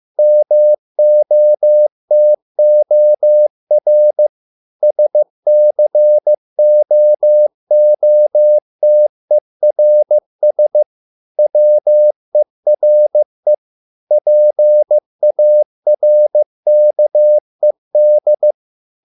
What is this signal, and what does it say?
Morse code, heard as a keyed tone, about 15 wpm, 600 Hz. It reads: MOTOR SCOOTERS WERE PARKED